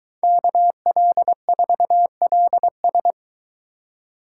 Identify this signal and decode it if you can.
Morse code, heard as a keyed tone, about 21 wpm, 700 Hz. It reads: KL4LS